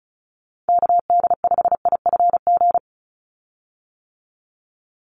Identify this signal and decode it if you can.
Morse code, heard as a keyed tone, about 34 wpm, 700 Hz. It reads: KD5IFG
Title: USA Call Signs 35wpm